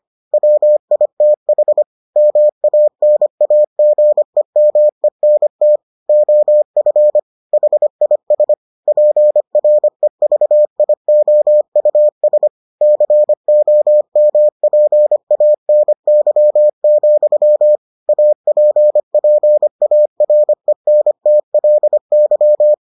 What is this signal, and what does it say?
Morse code, heard as a keyed tone, about 25 wpm, 600 Hz. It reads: WITH MANAGEMENT OF HIS PREVIOUS COMPANY, APPARENTLY